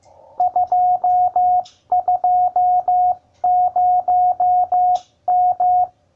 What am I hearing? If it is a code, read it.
Morse code, heard as a keyed tone, about 15 wpm, 720 Hz. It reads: 220M